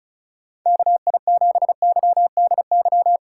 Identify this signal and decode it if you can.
Morse code, heard as a keyed tone, about 36 wpm, 700 Hz. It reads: KI7YDY
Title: USA Call Signs - 6 Characters - 2x Character Spacing - 35wpm